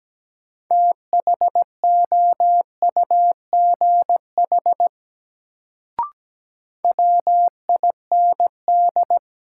Morse code, heard as a keyed tone, about 17 wpm, 700 Hz. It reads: THOUGH WIND
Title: Sentences from Top 400 Words - New Word Drill - 17wpm